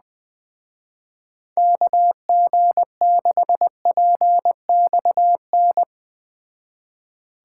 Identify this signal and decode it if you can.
Morse code, heard as a keyed tone, about 20 wpm, 700 Hz. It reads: KG6PXN